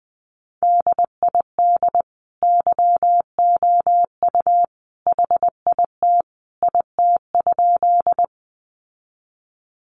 Morse code, heard as a keyed tone, about 20 wpm, 700 Hz. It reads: DID YOU HIT IT?
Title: Sentences from Top 1200 Words 20wpm